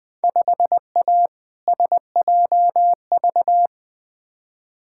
Morse code, heard as a keyed tone, about 20 wpm, 700 Hz. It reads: 5A SJV
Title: Field Day Exchanges - 20wpm